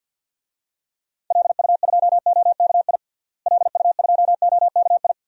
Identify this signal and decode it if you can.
Morse code, heard as a keyed tone, about 51 wpm, 700 Hz. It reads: LU2OKI LU2OKI
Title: DXCC-100 Call Signs - 6 Characters - Repeated 3x - 50wpm